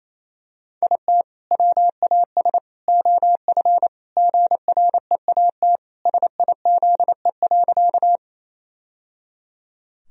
Morse code, heard as a keyed tone, about 27 wpm, 700 Hz. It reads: IT WAS OF GREAT SIZE.
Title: Sentences from Top 1000 Words - Review Part 2 - 28wpm